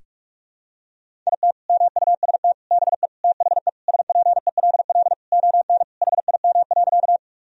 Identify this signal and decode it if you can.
Morse code, heard as a keyed tone, about 46 wpm, 700 Hz. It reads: IT MUST BE THE SPELL ON HIM.
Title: Sentences from Top 1000 Words - Review Part 1 - 45wpm